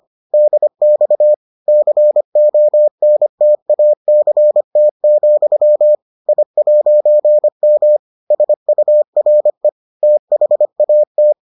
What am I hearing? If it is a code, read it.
Morse code, heard as a keyed tone, about 25 wpm, 600 Hz. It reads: DX CONTACT, I'M SURE THAT